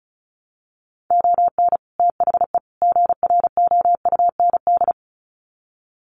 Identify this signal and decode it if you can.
Morse code, heard as a keyed tone, about 34 wpm, 700 Hz. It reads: ON THE GROUND
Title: Sets of 3 Words 35wpm